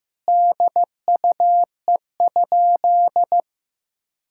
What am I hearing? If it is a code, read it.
Morse code, heard as a keyed tone, about 15 wpm, 700 Hz. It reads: DUE?
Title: Sentences from Top 1000 Words - Review Part 6 - 15wpm